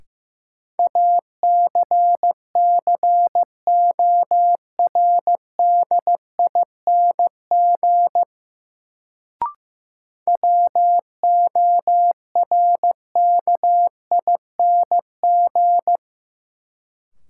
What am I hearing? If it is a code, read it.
Morse code, heard as a keyed tone, about 15 wpm, 700 Hz. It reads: ACCORDING WORKING
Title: Sentences from Top 700 Words - New Word Drill - 15wpm